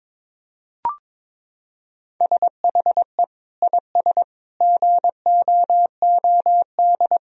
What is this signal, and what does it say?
Morse code, heard as a keyed tone, about 21 wpm, 700 Hz. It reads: SHE IS GOOD
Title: Sentences from Top 1000 Words - Review Part 3 - 22wpm